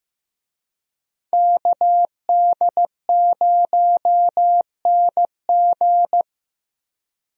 Morse code, heard as a keyed tone, about 15 wpm, 700 Hz. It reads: KD0NG
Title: USA Call Signs - 5 Characters - Repeated 3x - 15wpm